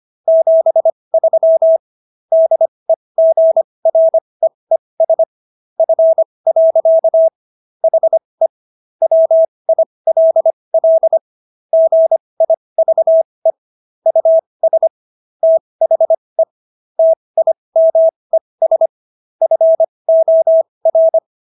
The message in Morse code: 73 DEGREES F. HE WILL GIVE US THE TIMES FOR

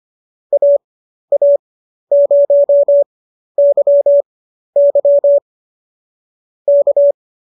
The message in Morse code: AA0YY K

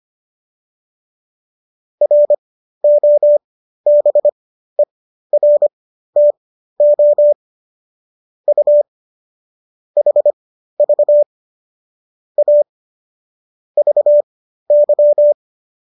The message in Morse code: ROBERTO U HV A VY

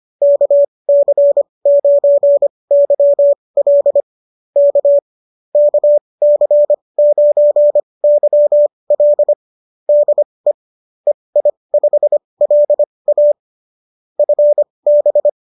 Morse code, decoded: KC9YL K KC9YL DE EI5LA FB

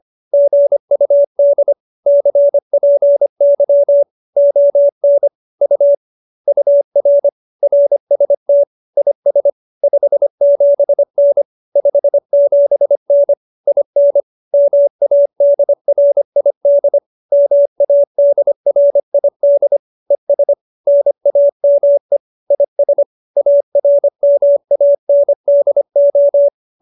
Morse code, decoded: GUD CPY ON U UR RST IS 57N 57N IN MADRID MADRID ES NAME IS ARMANDO

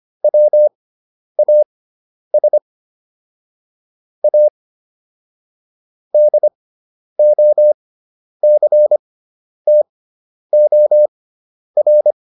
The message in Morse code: WAS A DOCTOR

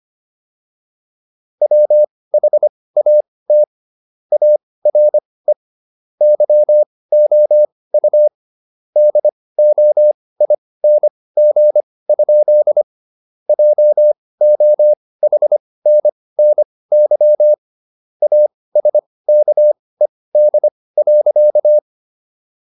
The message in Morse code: WHAT ARE YOU DOING? JOHNNY ASKED.

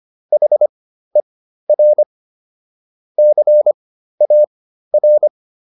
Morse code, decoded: HER CAR

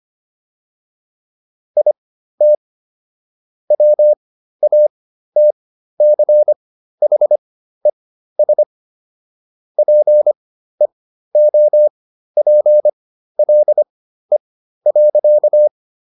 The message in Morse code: IT WATCHES PEOPLE.

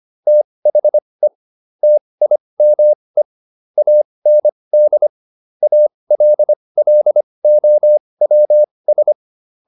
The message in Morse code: THE TIME AND ALLOWS

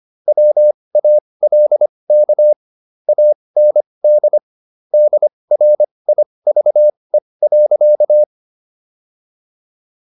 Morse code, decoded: WALK AND DRIVE.

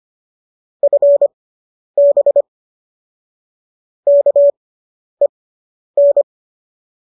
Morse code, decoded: FB KEN

